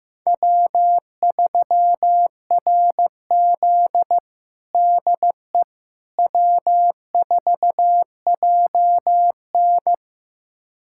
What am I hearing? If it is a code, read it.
Morse code, heard as a keyed tone, about 15 wpm, 700 Hz. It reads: W3RZ DE W4JN
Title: USA Callsign DE Callsign - 4 Characters - 15wpm